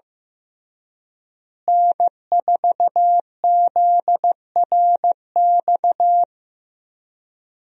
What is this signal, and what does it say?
Morse code, heard as a keyed tone, about 15 wpm, 700 Hz. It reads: N4ZRX
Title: USA Call Signs - 5 Characters - Repeated 3x - 15wpm